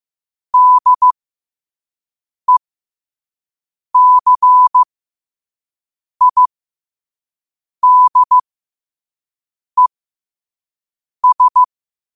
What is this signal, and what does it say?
Morse code, heard as a keyed tone, about 15 wpm, spaced as for 5 wpm, 1000 Hz. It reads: DECIDES
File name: qotdfiles / 5wpm